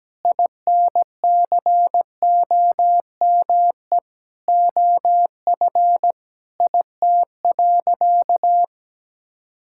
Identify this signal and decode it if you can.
Morse code, heard as a keyed tone, about 17 wpm, 700 Hz. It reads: INCOME OF IT.
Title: Sentences from Top 1000 Words - Review Part 6 - 17wpm